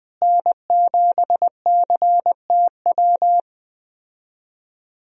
Morse code, heard as a keyed tone, about 20 wpm, 700 Hz. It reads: N7CTW
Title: USA Call Signs - 5 Characters - Repeated 3x - 20wpm